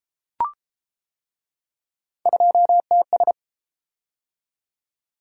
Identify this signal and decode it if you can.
Morse code, heard as a keyed tone, about 31 wpm, 700 Hz. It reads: 2TS